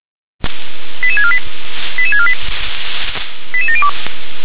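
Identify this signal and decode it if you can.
Five-tone selective call (ZVEI), heard as quick runs of five tones, tones of about 70 ms. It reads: 88648 88648 88702
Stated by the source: ZVEI five-tone selective call received by SDR.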